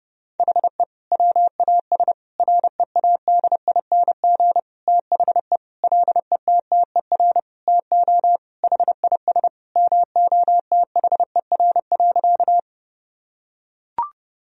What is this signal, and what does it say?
Morse code, heard as a keyed tone, about 30 wpm, 700 Hz. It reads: HE WAS READING THE LETTER TO HIS MOTHER.
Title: Sentences from Top 1000 Words - Review Part 3 - 30wpm